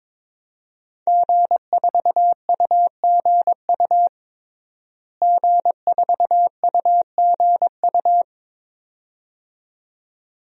Morse code, heard as a keyed tone, about 22 wpm, 700 Hz. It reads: G4UGU G4UGU
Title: DXCC-100 Call Signs - 5 Characters - Repeated 2x - 22wpm